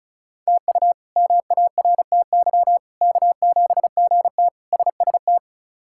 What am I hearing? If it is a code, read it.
Morse code, heard as a keyed tone, about 34 wpm, 700 Hz. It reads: TU MARTY K7GT SST
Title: K1USN SST QSO Practice - 35wpm